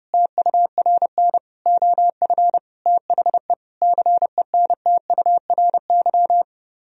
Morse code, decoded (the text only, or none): TURN OF THE CENTURY